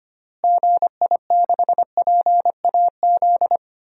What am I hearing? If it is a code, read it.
Morse code, heard as a keyed tone, about 25 wpm, 700 Hz. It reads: GI6PAZ
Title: DXCC-100 Call Signs - 6 Characters - Repeated 3x - 25wpm